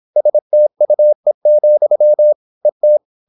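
Morse code, heard as a keyed tone, about 26 wpm, 600 Hz. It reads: STUE, ET
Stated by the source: High-speed (QRQ) Morse at 26 wpm.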